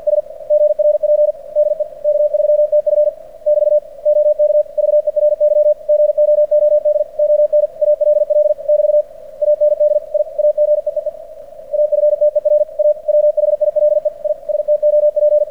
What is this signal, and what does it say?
Morse code, heard as a keyed tone, about 46 wpm, 590 Hz. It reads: S GRP DE K3TF <BT> OK VERY GOOD ON ALL <BT> AND I AGEEE AFTER A WHILE I STOP
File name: archived / CRSnet